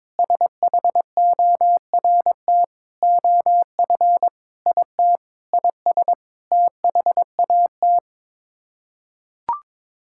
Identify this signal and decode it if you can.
Morse code, heard as a keyed tone, about 22 wpm, 700 Hz. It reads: SHORT OF IT IS THAT
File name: Sets of 9 Words - 22wpm